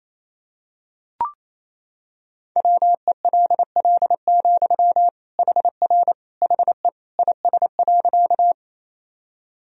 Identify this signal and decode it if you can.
Morse code, heard as a keyed tone, about 27 wpm, 700 Hz. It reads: WELL, HR HE IS.
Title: Sentences from Top 200 Words and Abbreviations 28wpm